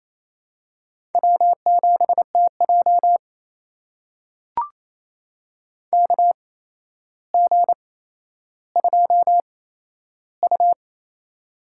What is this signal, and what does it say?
Morse code, heard as a keyed tone, about 28 wpm, spaced as for 12 wpm, 700 Hz. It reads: W7TJ KG2U
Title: USA Call Signs - 4 Characters - 8x Character Spacing - 28wpm